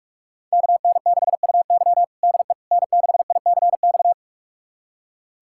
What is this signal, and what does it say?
Morse code, heard as a keyed tone, about 46 wpm, 700 Hz. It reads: KN6UY DE N6ICX
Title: USA Callsign DE Callsign - 5 Characters - 45wpm